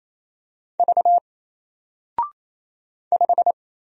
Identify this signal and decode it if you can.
Morse code, heard as a keyed tone, about 28 wpm, 700 Hz. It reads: V 5
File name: Single-Letter-Number-28wpm